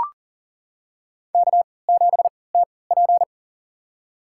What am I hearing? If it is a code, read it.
Morse code, heard as a keyed tone, about 40 wpm, spaced as for 23 wpm, 700 Hz. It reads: K7TP
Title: USA Call Signs - 4 Characters - 3x Character Spacing - 40wpm